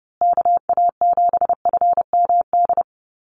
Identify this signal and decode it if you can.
Morse code, heard as a keyed tone, about 30 wpm, 700 Hz. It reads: KA7FMD
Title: USA Call Signs Repeated 2x 30wpm